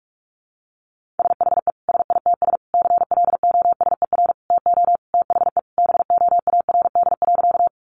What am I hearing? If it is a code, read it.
Morse code, heard as a keyed tone, about 44 wpm, 700 Hz. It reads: SHE SITS CLOSER TO THE BOARD.
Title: Additional Easy Phrases 45wpm